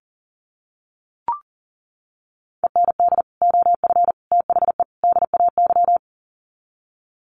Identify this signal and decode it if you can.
Morse code, heard as a keyed tone, about 40 wpm, 700 Hz. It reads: END OF THE DAY
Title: Sets of 4 Words 40wpm